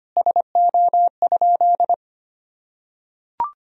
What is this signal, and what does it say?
Morse code, heard as a keyed tone, about 25 wpm, 700 Hz. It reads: SO?
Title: Sentences from Top 300 Words and Abbreviations 25wpm